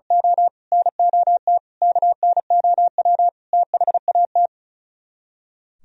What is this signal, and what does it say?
Morse code, heard as a keyed tone, about 35 wpm, 700 Hz. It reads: O NOT KNOW THAT
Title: Speed-Racing - Sets of 5 Words 35wpm